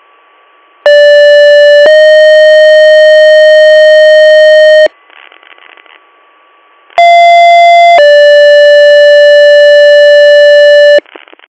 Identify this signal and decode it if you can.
Two-tone page, heard as a one-second tone, then a three-second tone, about 580 then 620 Hz, about 690 then 580 Hz, with noise. It is heard from 0.9 to 11.0 s.